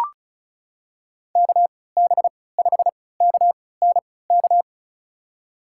Morse code, heard as a keyed tone, about 34 wpm, spaced as for 20 wpm, 700 Hz. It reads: KB5KNK
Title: USA Call Signs - 3x Character Spacing - 35wpm